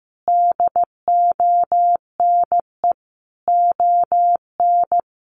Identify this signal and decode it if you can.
Morse code, heard as a keyed tone, about 15 wpm, 700 Hz. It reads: DONE ON
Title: Sentences from Top 800 Words 15wpm